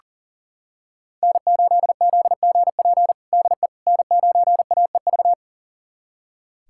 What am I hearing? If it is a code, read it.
Morse code, heard as a keyed tone, about 40 wpm, 700 Hz. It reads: N8ZGP DE N9AEV
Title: USA Callsign DE Callsign - 5 Characters - 40wpm